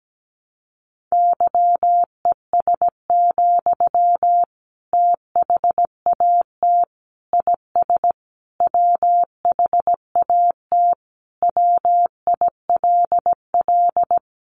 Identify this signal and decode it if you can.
Morse code, heard as a keyed tone, about 17 wpm, 700 Hz. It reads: YES, THAT IS WHAT WILL
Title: Sentences from Top 500 Words 17wpm